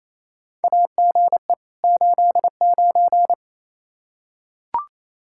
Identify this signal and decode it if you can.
Morse code, heard as a keyed tone, about 28 wpm, 700 Hz. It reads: AGE 89
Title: K1USN SST QSO Practice 28wpm